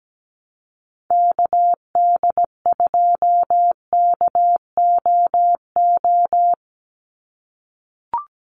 Morse code, heard as a keyed tone, about 17 wpm, 700 Hz. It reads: KD2KOO
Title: USA Call Signs 17wpm